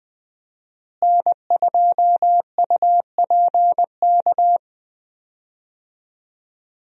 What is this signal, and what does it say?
Morse code, heard as a keyed tone, about 20 wpm, 700 Hz. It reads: N2UPK
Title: USA Call Signs - 5 Characters - Repeated 3x - 20wpm